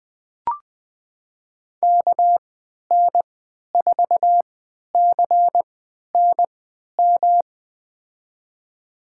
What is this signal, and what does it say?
Morse code, heard as a keyed tone, about 21 wpm, spaced as for 11 wpm, 700 Hz. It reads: KN4CNM
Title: USA Call Signs - 6 Characters - 3x Character Spacing - 20wpm